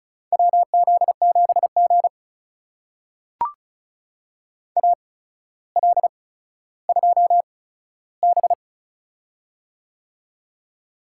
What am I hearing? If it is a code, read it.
Morse code, heard as a keyed tone, about 36 wpm, spaced as for 15 wpm, 700 Hz. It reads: WZ7G AL2B
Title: USA Call Signs - 4 Characters - 8x Character Spacing - 35wpm